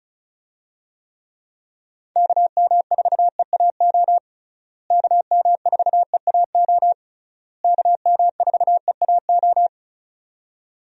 Morse code, heard as a keyed tone, about 36 wpm, 700 Hz. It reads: KM4EAO KM4EAO KM4EAO
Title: USA Call Signs - 6 Characters - Repeated 3x - 35wpm